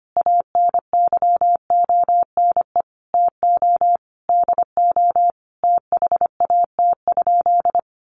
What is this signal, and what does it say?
Morse code, heard as a keyed tone, about 25 wpm, 700 Hz. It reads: ANYONE TO DO THAT?
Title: Sentences from Top 900 Words 25wpm